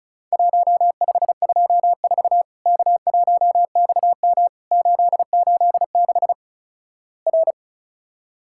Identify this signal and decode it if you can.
Morse code, heard as a keyed tone, about 36 wpm, 700 Hz. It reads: 1524 K1XM 886 R
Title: Worked All Europe DX Contest - QTC Exchange - 35wpm